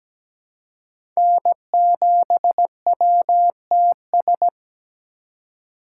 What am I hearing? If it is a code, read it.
Morse code, heard as a keyed tone, about 17 wpm, 700 Hz. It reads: N7WTS
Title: USA Call Signs - 5 Characters - 2x Character Spacing - 17wpm